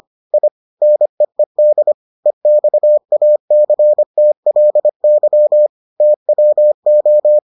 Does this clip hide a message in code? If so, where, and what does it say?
Yes: Morse code from 0.3 to 7.4 s, heard as a keyed tone, about 25 wpm, 600 Hz. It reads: I NEED EXACTLY TWO